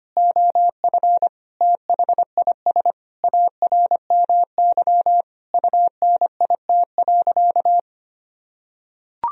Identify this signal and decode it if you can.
Morse code, heard as a keyed tone, about 25 wpm, 700 Hz. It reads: OF THIS ARMY UNIT.
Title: Sentences from Top 1000 Words - Review Part 2 - 25wpm